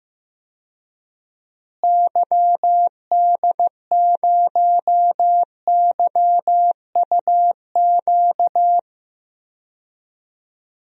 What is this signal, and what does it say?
Morse code, heard as a keyed tone, about 15 wpm, 700 Hz. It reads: YD0YUQ
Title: DXCC-100 Call Signs - 6 Characters - Repeated 2x - 15wpm